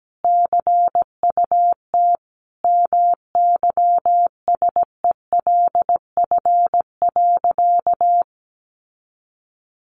Morse code, heard as a keyed tone, about 17 wpm, 700 Hz. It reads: CUT MYSELF.
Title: Sentences from Top 1300 Words 17wpm